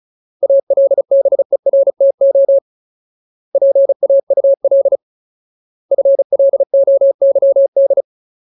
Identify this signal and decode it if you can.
Morse code, heard as a keyed tone, about 34 wpm, 550 Hz. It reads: ALBERTO PAUL FLOYD
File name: Ultimate Word List / 35WPM Ultimate